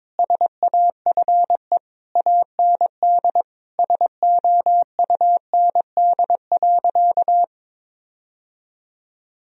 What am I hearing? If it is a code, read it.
Morse code, heard as a keyed tone, about 21 wpm, 700 Hz. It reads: SAFE AND SOUND.
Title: Sentences from Top 1000 Words - Review Part 1 - 22wpm